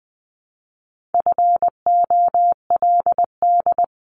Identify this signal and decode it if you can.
Morse code, heard as a keyed tone, about 20 wpm, 700 Hz. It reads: FOLD
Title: Multi-letter Phonemes Repeated 3x 20wpm